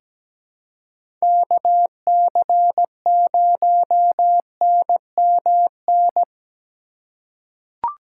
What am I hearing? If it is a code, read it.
Morse code, heard as a keyed tone, about 17 wpm, 700 Hz. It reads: KC0NMN
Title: USA Call Signs - 6 Characters - 4x Character Spacing - 17wpm